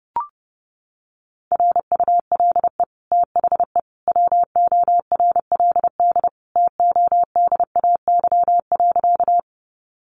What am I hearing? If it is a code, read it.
Morse code, heard as a keyed tone, about 30 wpm, 700 Hz. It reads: RULE THE WORLD TODAY.
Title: Sentences from Top 600 Words 30wpm